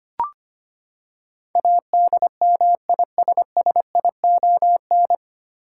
Morse code, heard as a keyed tone, about 25 wpm, 700 Hz. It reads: ADMISSION
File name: Nine Letter Words - 25wpm